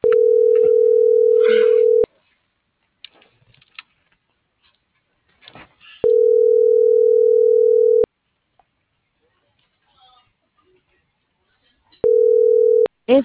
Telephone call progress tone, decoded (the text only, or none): ringback tone